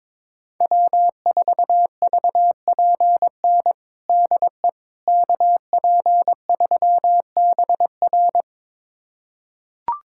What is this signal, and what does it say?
Morse code, heard as a keyed tone, about 22 wpm, 700 Hz. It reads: W4VPN DE KP3BR
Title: USA Callsign DE Callsign - 5 Characters - 22wpm